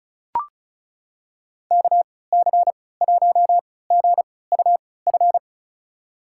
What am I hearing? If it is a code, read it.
Morse code, heard as a keyed tone, about 36 wpm, spaced as for 20 wpm, 700 Hz. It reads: KC1GUF
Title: USA Call Signs - 6 Characters - 3x Character Spacing - 35wpm